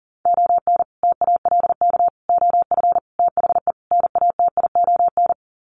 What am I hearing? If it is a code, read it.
Morse code, heard as a keyed tone, about 40 wpm, 700 Hz. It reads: ON TALK OF THE NATION
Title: Sets of 5 Words 40wpm